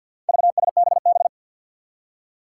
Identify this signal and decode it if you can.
Morse code, heard as a keyed tone, about 46 wpm, 700 Hz. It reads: VSBB
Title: Four Challenging Letter-Number Combinations 50wpm